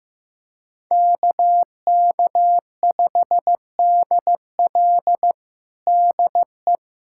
Morse code, heard as a keyed tone, about 15 wpm, 700 Hz. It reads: KK5DL DE
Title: USA Callsign DE Callsign 15wpm